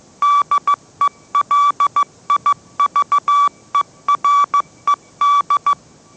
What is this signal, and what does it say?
Morse code, heard as a keyed tone, about 15 wpm, 1150 Hz. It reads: DELIVERED
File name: mrx / mrx065 16wpm